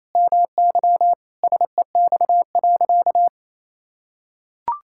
Morse code, heard as a keyed tone, about 28 wpm, 700 Hz. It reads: MY SEX.
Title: Sentences from Top 1600 Words 28wpm